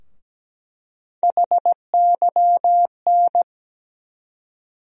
Morse code, heard as a keyed tone, about 17 wpm, 700 Hz. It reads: HYN
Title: USA Call Sign Postfixes - 3 Letters 17wpm